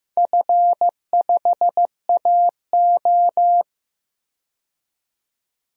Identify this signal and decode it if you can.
Morse code, heard as a keyed tone, about 15 wpm, 700 Hz. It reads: F5AO
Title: DXCC-100 Call Signs - 4 Characters - 15wpm